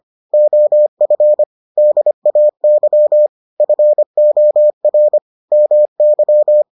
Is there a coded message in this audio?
Yes: Morse code from 0.3 to 6.6 s, heard as a keyed tone, about 25 wpm, 600 Hz. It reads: OF DAY FOR MY